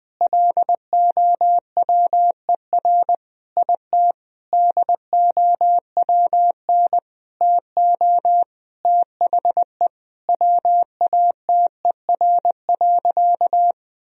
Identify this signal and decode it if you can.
Morse code, heard as a keyed tone, about 19 wpm, 700 Hz. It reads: LOWER IT DOWN TO THE WATER.
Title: Sentences from Top 1000 Words - Review Part 4 - 20wpm